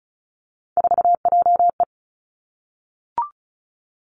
Morse code, heard as a keyed tone, about 36 wpm, 700 Hz. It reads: 4JE
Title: Three Challenging Letter-Number Combinations 35wpm